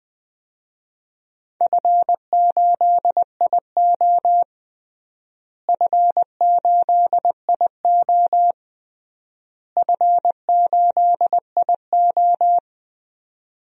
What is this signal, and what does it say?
Morse code, heard as a keyed tone, about 21 wpm, 690 Hz. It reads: F8IO F8IO F8IO
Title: DXCC-100 Call Signs - 4 Characters - Repeated 3x - 20wpm